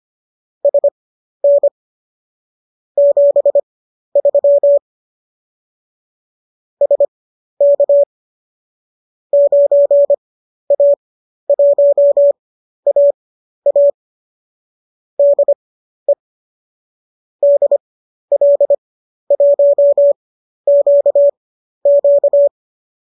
Morse code, heard as a keyed tone, about 25 wpm, spaced as for 12 wpm, 570 Hz. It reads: SN 73 SK 9A1AA DE DL1QQ
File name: qso012 12